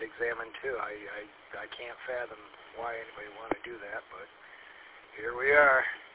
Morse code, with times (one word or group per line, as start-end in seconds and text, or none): none